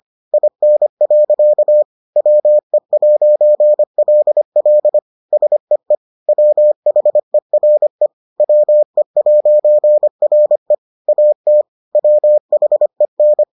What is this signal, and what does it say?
Morse code, heard as a keyed tone, about 25 wpm, 600 Hz. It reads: IN. WE'LL SEE WHERE WE'RE AT WHEN